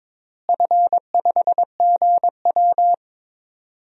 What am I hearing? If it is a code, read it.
Morse code, heard as a keyed tone, about 22 wpm, 700 Hz. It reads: F5GW
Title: DXCC-100 Call Signs - 4 Characters - Repeated 3x - 22wpm